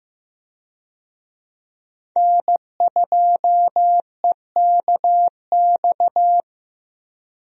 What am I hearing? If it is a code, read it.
Morse code, heard as a keyed tone, about 15 wpm, 700 Hz. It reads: N2EKX
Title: USA Call Signs - 5 Characters - Repeated 2x - 15wpm